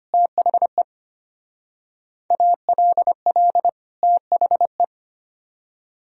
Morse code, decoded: THE ALL THE

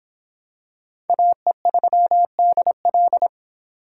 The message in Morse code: AE3DL

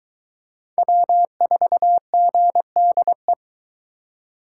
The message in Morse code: W4GDE

